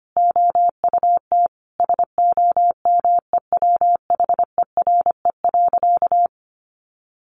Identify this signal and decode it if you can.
Morse code, heard as a keyed tone, about 25 wpm, 700 Hz. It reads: OUT SOMEWHERE.
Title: Sentences from Top 1400 Words 25wpm